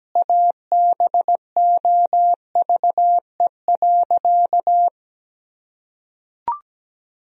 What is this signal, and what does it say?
Morse code, heard as a keyed tone, about 17 wpm, 700 Hz. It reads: ABOVE.
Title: Sentences from Top 1000 Words - Review Part 2 - 17wpm